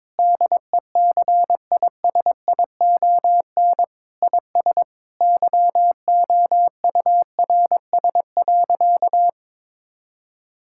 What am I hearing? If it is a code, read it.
Morse code, heard as a keyed tone, about 22 wpm, 700 Hz. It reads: DECISION IS YOURS.
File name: practice / 22wpm / Sentences with 8-Word Phrases - 22wpm